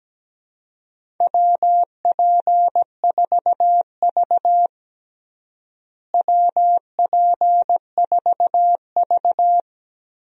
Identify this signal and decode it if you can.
Morse code, heard as a keyed tone, about 17 wpm, 700 Hz. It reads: WP4V WP4V
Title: USA Call Signs - 4 Characters - Repeated 3x - 17wpm